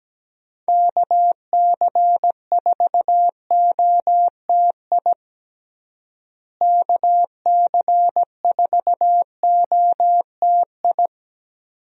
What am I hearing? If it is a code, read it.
Morse code, heard as a keyed tone, about 17 wpm, 700 Hz. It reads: KC4OTI KC4OTI
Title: USA Call Signs - 6 Characters - Repeated 2x - 17wpm